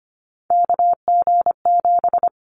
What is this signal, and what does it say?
Morse code, heard as a keyed tone, about 25 wpm, 700 Hz. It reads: KG7
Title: Callsign Prefixes for Top 25 DXCC - Repeated 3x - 25wpm